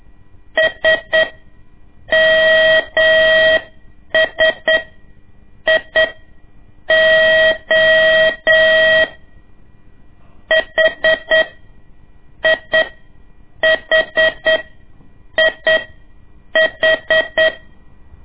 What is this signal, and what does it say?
Morse code, heard as a keyed tone, about 8 wpm, spaced as for 6 wpm, 640 Hz. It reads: SMSIO HIHIH
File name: ik1weg2 / CORSI CW / lez2-1